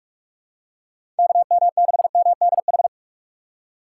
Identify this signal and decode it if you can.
Morse code, heard as a keyed tone, about 42 wpm, 700 Hz. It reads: KM6MDH
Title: USA Call Signs - 2x Character Spacing - 45wpm